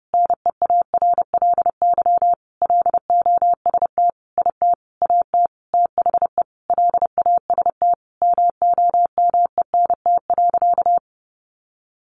Morse code, heard as a keyed tone, about 30 wpm, 700 Hz. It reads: NEARLY LOST IT AT THE LAST MOMENT.
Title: Sentences from Top 1400 Words 30wpm